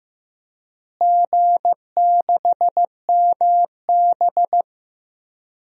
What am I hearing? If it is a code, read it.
Morse code, heard as a keyed tone, about 15 wpm, 700 Hz. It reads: G6MB
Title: DXCC-100 Call Signs - 4 Characters - Repeated 2x - 15wpm